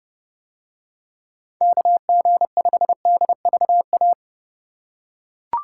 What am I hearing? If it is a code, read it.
Morse code, heard as a keyed tone, about 29 wpm, 700 Hz. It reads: KG5DVA